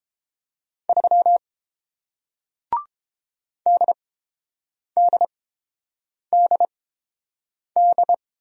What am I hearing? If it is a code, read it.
Morse code, heard as a keyed tone, about 31 wpm, spaced as for 7 wpm, 700 Hz. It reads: 3 DDDD